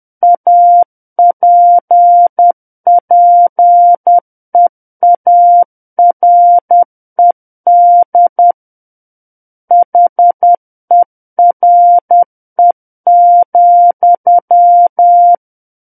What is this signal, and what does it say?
Morse code, heard as a keyed tone, about 10 wpm, 700 Hz. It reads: APPEARED HERE,
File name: Book One / 01-Chapter 1 - 10 WPM